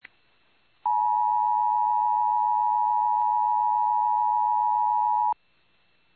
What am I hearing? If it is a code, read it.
Emergency Alert System (SAME) tones (the 853 and 960 Hz attention tone). It is heard from 0.8 to 5.4 s.